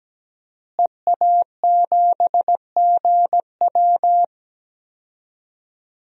Morse code, heard as a keyed tone, about 17 wpm, 700 Hz. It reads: EA7GW
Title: DXCC-100 Call Signs - 5 Characters - Repeated 2x - 17wpm